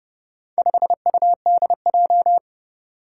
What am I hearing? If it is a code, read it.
Morse code, heard as a keyed tone, about 28 wpm, 700 Hz. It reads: 5UDJ
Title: Four Challenging Letter-Number Combinations 30wpm